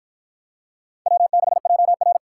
Morse code, heard as a keyed tone, about 48 wpm, 700 Hz. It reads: W6JR